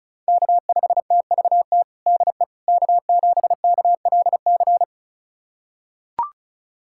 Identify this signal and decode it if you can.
Morse code, heard as a keyed tone, about 36 wpm, 700 Hz. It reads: K5TVT DE K7KLC
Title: USA Callsign DE Callsign - 5 Characters - 35wpm